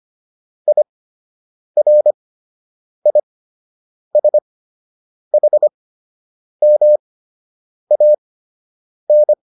Morse code, heard as a keyed tone, about 25 wpm, spaced as for 8 wpm, 600 Hz. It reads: IRISHMAN